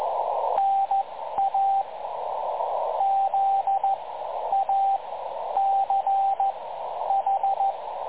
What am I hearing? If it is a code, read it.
Morse code, heard as a keyed tone, about 14 wpm, spaced as for 10 wpm, 780 Hz, with radio noise. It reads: NA ZACH
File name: dlazaawansowanych / nagraniazpasma / websdr recording 3565.0kHz